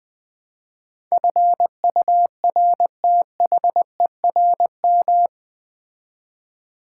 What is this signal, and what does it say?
Morse code, heard as a keyed tone, about 20 wpm, 700 Hz. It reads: FURTHERM
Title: Build A Word - 4 Letters At A Time - 11-Letter Words - 20wpm